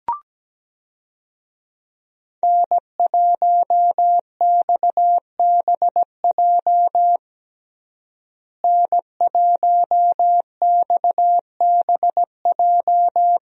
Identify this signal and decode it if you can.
Morse code, heard as a keyed tone, about 17 wpm, 700 Hz. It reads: N1XBJ N1XBJ
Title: USA Call Signs - 5 Characters - Repeated 2x - 17wpm